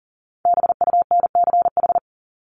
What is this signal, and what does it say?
Morse code, heard as a keyed tone, about 40 wpm, 700 Hz. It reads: BUNCH
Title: Top 1751-2000 Words 40wpm